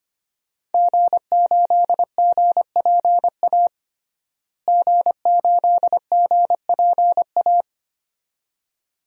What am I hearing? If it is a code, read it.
Morse code, heard as a keyed tone, about 25 wpm, 700 Hz. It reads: G8GPA G8GPA
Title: DXCC-100 Call Signs - 5 Characters - Repeated 2x - 25wpm